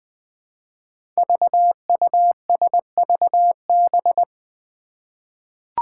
Morse code, heard as a keyed tone, about 20 wpm, 700 Hz. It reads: VUSVB
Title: Five Challenging Letter-Number Combinations 20wpm